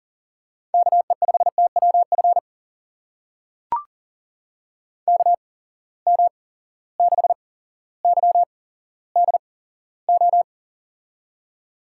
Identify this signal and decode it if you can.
Morse code, heard as a keyed tone, about 43 wpm, spaced as for 17 wpm, 700 Hz. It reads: KE5TWF KM6YDO
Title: USA Call Signs - 6 Characters - 8x Character Spacing - 40wpm